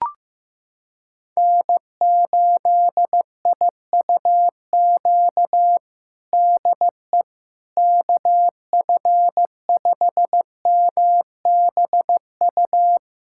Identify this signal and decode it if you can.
Morse code, heard as a keyed tone, about 15 wpm, 700 Hz. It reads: N8IUQ DE KF5MBU
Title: USA Callsign DE Callsign 15wpm